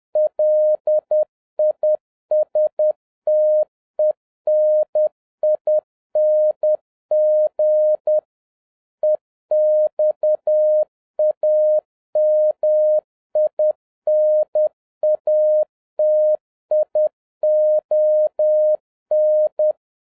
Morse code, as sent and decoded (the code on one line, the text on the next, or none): .-.. .. ... - . -. .. -. --. / . -..- .- -- .. -. .- - .. --- -.
LISTENING EXAMINATION